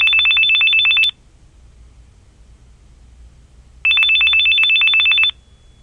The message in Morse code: none